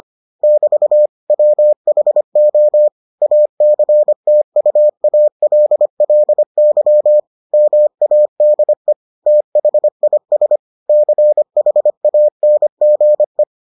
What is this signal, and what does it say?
Morse code, heard as a keyed tone, about 25 wpm, 600 Hz. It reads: <BT> WHO ACTUALLY MADE THIS CHANGE